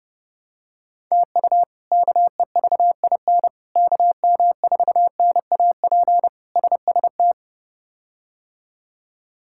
Morse code, heard as a keyed tone, about 32 wpm, 700 Hz. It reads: TU KEVIN KM4NAP SST
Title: K1USN SST QSO Practice - 30wpm